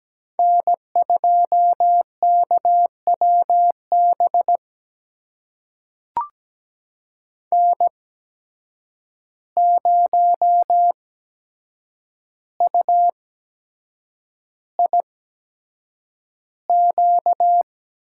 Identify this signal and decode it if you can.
Morse code, heard as a keyed tone, about 17 wpm, spaced as for 7 wpm, 700 Hz. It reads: N2KWB N0UIQ
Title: USA Call Signs - 5 Characters - 8x Character Spacing - 17wpm